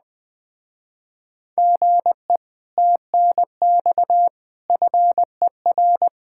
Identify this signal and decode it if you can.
Morse code, heard as a keyed tone, about 20 wpm, 700 Hz. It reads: GE TNX FER
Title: Speed-Racing Common QSO Phrases - 17wpm